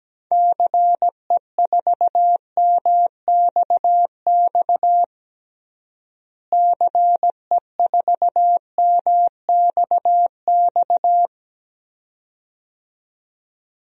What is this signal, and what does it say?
Morse code, heard as a keyed tone, about 17 wpm, 700 Hz. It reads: CE4MXX CE4MXX
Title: DXCC-100 Call Signs - 6 Characters - Repeated 2x - 17wpm